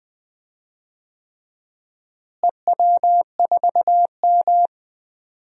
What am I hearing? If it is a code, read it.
Morse code, heard as a keyed tone, about 21 wpm, 700 Hz. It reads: EW4M
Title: DXCC-100 Call Signs - 4 Characters - Repeated 3x - 20wpm